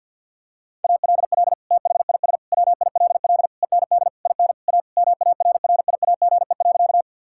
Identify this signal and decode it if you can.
Morse code, heard as a keyed tone, about 51 wpm, 690 Hz. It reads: ALL THIS WILL END IN A MARRIAGE.
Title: Sentences from Top 1000 Words - Review Part 5 - 50wpm